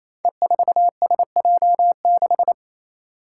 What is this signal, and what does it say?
Morse code, heard as a keyed tone, about 28 wpm, 700 Hz. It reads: E4SJ6
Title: Five Challenging Letter-Number Combinations 28wpm